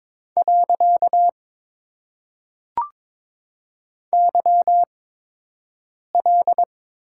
Morse code, heard as a keyed tone, about 22 wpm, 700 Hz. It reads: . Y L
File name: Top 90 QSO Elements - 8x Character Spacing - 22wpm